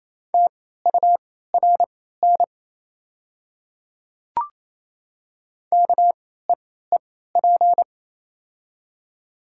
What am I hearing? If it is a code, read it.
Morse code, heard as a keyed tone, about 28 wpm, spaced as for 16 wpm, 700 Hz. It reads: TURN KEEP
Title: 3x Character Spacing - Top 300 Words - 28wpm